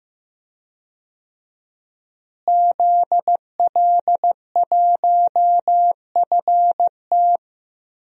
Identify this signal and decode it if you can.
Morse code, heard as a keyed tone, about 16 wpm, 700 Hz. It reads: ZL1FT
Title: DXCC-100 Call Signs - 5 Characters - Repeated 2x - 15wpm